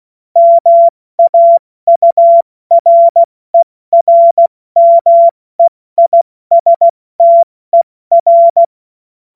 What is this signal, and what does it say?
Morse code, heard as a keyed tone, about 16 wpm, 680 Hz. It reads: MAURERMEISTER
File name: Morsekurs / Witze / Witz 09. Dezember 2025 70 F80